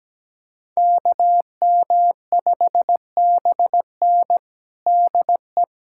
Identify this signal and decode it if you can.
Morse code, heard as a keyed tone, about 17 wpm, 700 Hz. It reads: KM5BN DE
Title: USA Callsign DE Callsign - 5 Characters - 17wpm